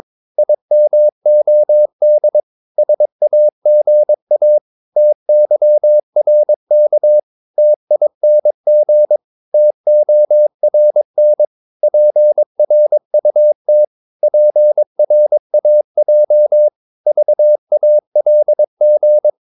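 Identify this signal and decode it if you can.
Morse code, heard as a keyed tone, about 22 wpm, 600 Hz. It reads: IMOD SAGA TYRK TING TORN PRUT PRAJ VALG
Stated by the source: Morse practice of short Danish words at 22 wpm, from CW med Gnister.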